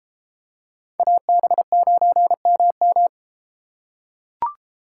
Morse code, heard as a keyed tone, about 33 wpm, 700 Hz. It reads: AB9MM